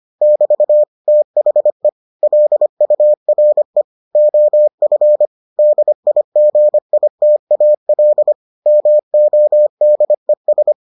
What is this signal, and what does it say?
Morse code, heard as a keyed tone, about 25 wpm, 600 Hz. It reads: <BT> THE LURE OF DIGITAL MODES